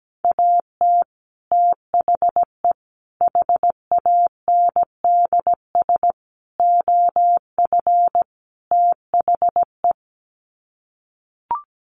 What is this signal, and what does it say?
Morse code, heard as a keyed tone, about 17 wpm, 700 Hz. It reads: AT THE HANDS OF THE
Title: Sets of 5 Words Encore2 17wpm